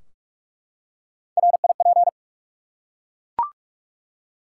Morse code, heard as a keyed tone, about 48 wpm, 700 Hz. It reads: RIP
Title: Three Letter Words - 45wpm